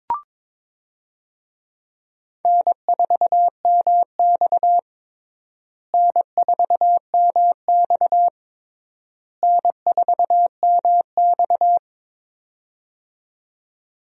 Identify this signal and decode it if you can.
Morse code, heard as a keyed tone, about 22 wpm, 700 Hz. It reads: N4MX N4MX N4MX
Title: USA Call Signs - 4 Characters - Repeated 3x - 22wpm